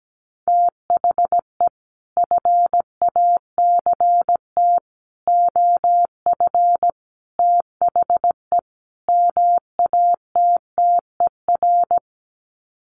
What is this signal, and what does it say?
Morse code, heard as a keyed tone, about 17 wpm, 700 Hz. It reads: THE FACT OF THE MATTER
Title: Sets of 5 Words 17wpm